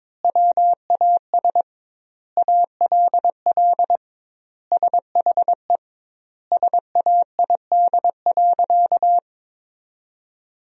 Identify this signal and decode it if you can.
Morse code, heard as a keyed tone, about 22 wpm, 700 Hz. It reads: WAS ALL SHE SAID.
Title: 2x Word-Spacing - Sentences from Top 100 Words - 22wpm